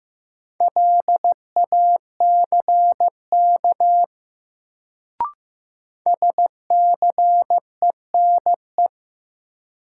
Morse code, heard as a keyed tone, about 15 wpm, 700 Hz. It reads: LACK SCENE
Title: Sentences from Top 1100 Words - New Word Drill Alt - 15wpm